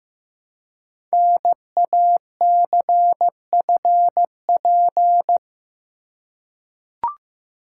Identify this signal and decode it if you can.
Morse code, heard as a keyed tone, about 15 wpm, 700 Hz. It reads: NACFP
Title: Code Groups - 5 Letter:Numbers - 15wpm